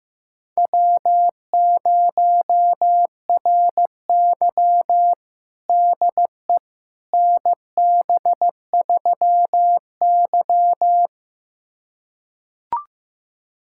Morse code, heard as a keyed tone, about 16 wpm, 700 Hz. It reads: W0RY DE NB3Y
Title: USA Callsign DE Callsign - 4 Characters - 15wpm